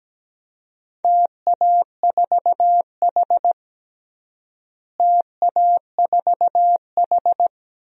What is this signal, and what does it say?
Morse code, heard as a keyed tone, about 17 wpm, 700 Hz. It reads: TA4H TA4H
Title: DXCC-100 Call Signs - 4 Characters - Repeated 3x - 17wpm